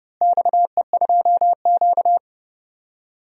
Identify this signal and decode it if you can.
Morse code, heard as a keyed tone, about 30 wpm, 700 Hz. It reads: XE2Q
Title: DXCC-100 Call Signs - 4 Characters - Repeated 3x - 30wpm